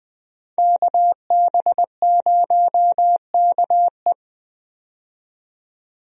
Morse code, heard as a keyed tone, about 20 wpm, 700 Hz. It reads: KB0KE
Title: USA Call Signs - 5 Characters - 20wpm